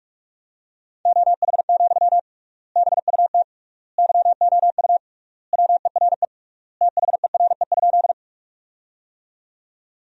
Morse code, heard as a keyed tone, about 46 wpm, 700 Hz. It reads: OH, BUT YOU WERE THERE?
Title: 3x Word-Spacing - Sentences from Top 500 Words - 45wpm